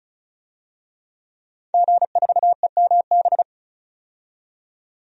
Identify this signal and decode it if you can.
Morse code, heard as a keyed tone, about 34 wpm, 700 Hz. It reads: G4EMB
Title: DXCC-100 Call Signs - 5 Characters - 35wpm